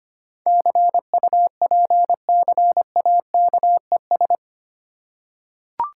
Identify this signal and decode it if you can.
Morse code, heard as a keyed tone, about 25 wpm, 700 Hz. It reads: CUPCAKES
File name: practice / 25wpm / Difficult Words with Hints - 25wpm